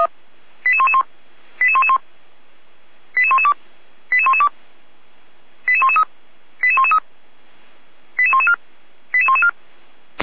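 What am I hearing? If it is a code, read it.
Five-tone selective call (ZVEI), heard as quick runs of five tones, tones of about 70 ms. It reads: 88111 88111 88112 88112 88113 88113 88114 88114
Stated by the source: ZVEI five-tone selective call 88497, received via SDR, archived by the Feuerwehr Orlamünde fire brigade.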